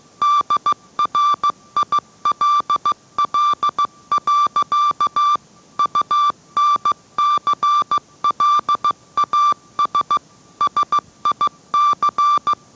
Morse code, heard as a keyed tone, about 15 wpm, 1170 Hz. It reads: DRILL. UNCLAS SIC